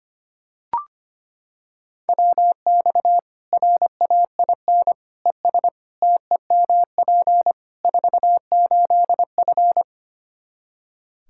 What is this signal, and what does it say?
Morse code, heard as a keyed tone, about 25 wpm, 700 Hz. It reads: WX RAIN ES TEMP 48F
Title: Practice QSO - Exchange 2 - Info - 25wpm